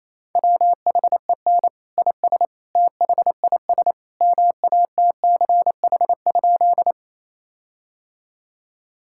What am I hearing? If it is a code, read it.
Morse code, heard as a keyed tone, about 27 wpm, 700 Hz. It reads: WHEN IS THIS MATCH?
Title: Sentences from Top 1600 Words 28wpm